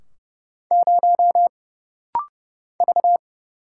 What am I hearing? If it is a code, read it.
Morse code, heard as a keyed tone, about 30 wpm, 700 Hz. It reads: 0 V
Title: Single Letter-Number Alt - Rapid-Fire 30wpm